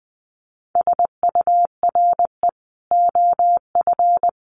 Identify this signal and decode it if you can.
Morse code, heard as a keyed tone, about 19 wpm, 700 Hz. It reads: SURE OF
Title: Sentences from Top 300 Words 20wpm